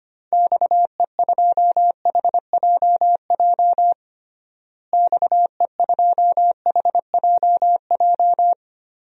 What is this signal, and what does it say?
Morse code, heard as a keyed tone, about 25 wpm, 700 Hz. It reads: XE2HJJ XE2HJJ
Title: DXCC-100 Call Signs - 6 Characters - Repeated 3x - 25wpm